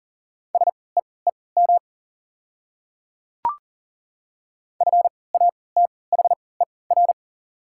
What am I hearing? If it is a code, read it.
Morse code, heard as a keyed tone, about 40 wpm, spaced as for 23 wpm, 690 Hz. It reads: SEEM FATHER
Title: 3x Character Spacing - Top 300 Words - 40wpm